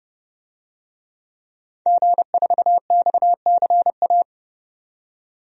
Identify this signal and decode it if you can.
Morse code, heard as a keyed tone, about 30 wpm, 700 Hz. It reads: G4XCA
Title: DXCC-100 Call Signs - 5 Characters - 30wpm